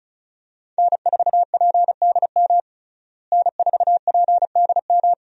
Morse code, heard as a keyed tone, about 35 wpm, 700 Hz. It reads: N4PDM N4PDM
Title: USA Call Signs - 5 Characters - Repeated 2x - 35wpm